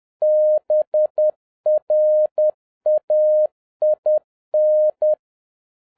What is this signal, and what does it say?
Morse code, heard as a keyed tone, about 10 wpm, 610 Hz. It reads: BRAIN
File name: 2011-07-mp3 / n56 BACK FROM A BIG ACCIDENT 10wpm